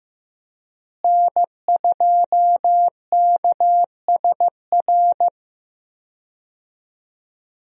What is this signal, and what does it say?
Morse code, heard as a keyed tone, about 15 wpm, 700 Hz. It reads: N2KSR
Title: USA Call Signs - 5 Characters - Repeated 3x - 15wpm